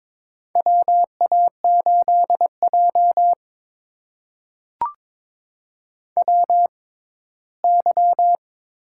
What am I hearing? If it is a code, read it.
Morse code, heard as a keyed tone, about 21 wpm, spaced as for 8 wpm, 700 Hz. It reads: WA8JWY
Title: USA Call Signs - 6x Character Spacing - 22wpm